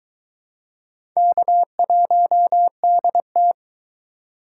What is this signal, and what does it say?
Morse code, heard as a keyed tone, about 21 wpm, 700 Hz. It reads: K1DT